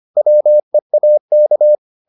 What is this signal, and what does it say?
Morse code, heard as a keyed tone, about 25 wpm, 600 Hz. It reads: WEAK